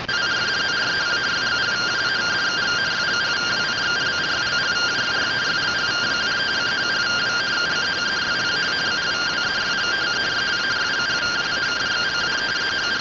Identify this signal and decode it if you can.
Two-tone teleprinter signal, about 170 Hz shift.